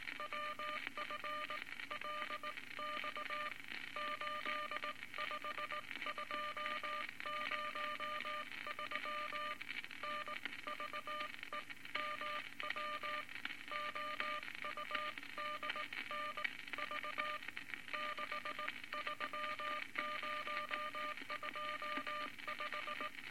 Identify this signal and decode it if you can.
Morse code, heard as a keyed tone, about 19 wpm, spaced as for 17 wpm, 1200 Hz, with noise. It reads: WFLX 85203 NVEMW OUDNV 63025
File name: Morse Code Records / 19 B LCM 167 Twelve Words Per Minute